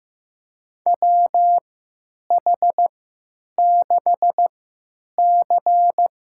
Morse code, heard as a keyed tone, about 15 wpm, spaced as for 9 wpm, 700 Hz. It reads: WH6C
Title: USA Call Signs - 3x Character Spacing - 15wpm